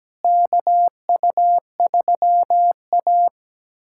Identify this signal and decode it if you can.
Morse code, heard as a keyed tone, about 17 wpm, 700 Hz. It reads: KU3A
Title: USA Callsign DE Callsign - 4 Characters - 17wpm